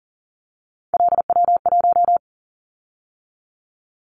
Morse code, heard as a keyed tone, about 38 wpm, 700 Hz. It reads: LW1
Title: Callsign Prefixes for Top 100 DXCC - 40wpm